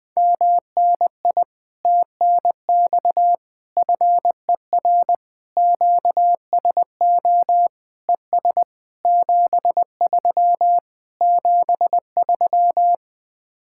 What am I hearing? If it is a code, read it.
Morse code, heard as a keyed tone, about 20 wpm, 700 Hz. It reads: MNI TNX FER QSO ES 73 73
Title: Speed-Racing Common QSO Phrases - 20wpm